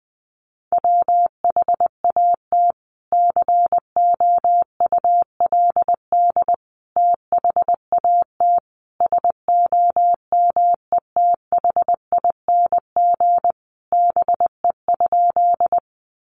Morse code, heard as a keyed tone, about 20 wpm, 700 Hz. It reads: WHAT COULD THAT SOMETHING BE?
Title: Sentences from Top 600 Words 20wpm